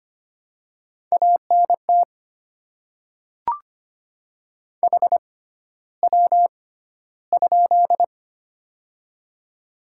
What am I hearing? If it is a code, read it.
Morse code, heard as a keyed tone, about 25 wpm, spaced as for 13 wpm, 700 Hz. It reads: ANT HW?
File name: Top 20 QSO Elements - 6x Character Spacing - 25wpm